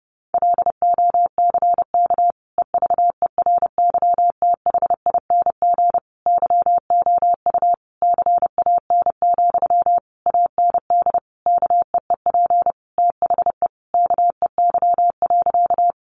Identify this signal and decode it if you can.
Morse code, heard as a keyed tone, about 30 wpm, 700 Hz. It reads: LOCK EVERYTHING YOU CAN, AND KEEP THE KEY.
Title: Sentences from Top 1300 Words 30wpm